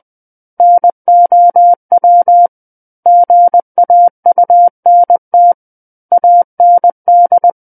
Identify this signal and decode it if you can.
Morse code, heard as a keyed tone, about 20 wpm, 700 Hz. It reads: NOW GAUNT AND
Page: SKCC Learning Center - Morse Code Practice Files